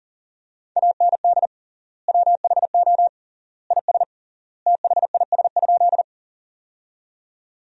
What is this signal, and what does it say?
Morse code, heard as a keyed tone, about 40 wpm, 700 Hz. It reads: AND WHO IS THIS?
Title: 3x Word-Spacing - Sentences from Top 100 Words - 40wpm